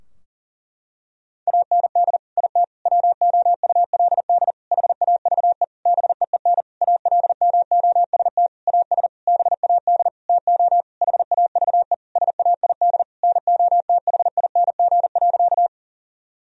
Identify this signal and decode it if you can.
Morse code, heard as a keyed tone, about 37 wpm, 700 Hz. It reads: AND IT WOULD HAVE BEEN ALMOST AS BAD TO HAVE SAID NOTHING.
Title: Sentences from Top 1000 Words - Review Part 4 - 40wpm